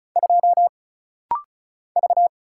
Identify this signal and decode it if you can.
Morse code, heard as a keyed tone, about 36 wpm, 700 Hz. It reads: 2 V
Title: Single Letter-Number Alt - Rapid-Fire 35wpm